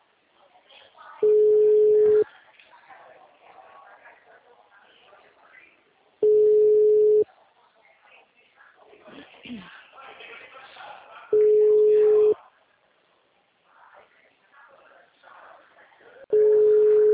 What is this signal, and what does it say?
Ringback tone (Europe, 425 Hz), with speech.